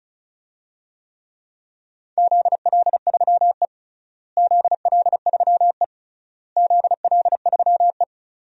Morse code, heard as a keyed tone, about 34 wpm, 700 Hz. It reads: ZL3E ZL3E ZL3E
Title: DXCC-100 Call Signs - 4 Characters - Repeated 3x - 35wpm